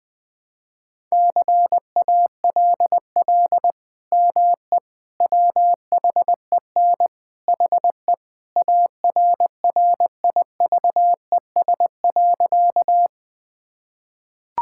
Morse code, heard as a keyed tone, about 20 wpm, 700 Hz. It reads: CALL ME WHEN HE ARRIVES.
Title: Sentences with 2-Word Phrases - Spoken Beginning - 20wpm